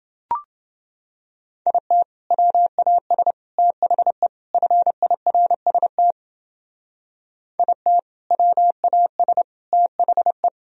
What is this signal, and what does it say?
Morse code, heard as a keyed tone, about 29 wpm, 700 Hz. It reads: IT WAS THE FIRST IT WAS THE